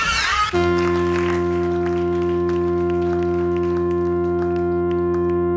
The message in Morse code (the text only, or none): none